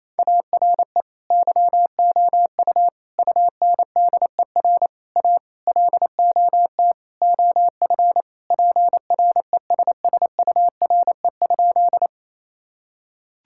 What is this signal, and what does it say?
Morse code, heard as a keyed tone, about 28 wpm, 700 Hz. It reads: ARE YOU UNDER A LOT OF PRESSURE?
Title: Sentences from Top 1000 Words - Review Part 6 - 28wpm